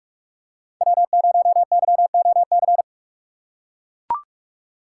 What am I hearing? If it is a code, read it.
Morse code, heard as a keyed tone, about 44 wpm, 700 Hz. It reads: W0YOC
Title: USA Call Signs - 6x Character Spacing - 45wpm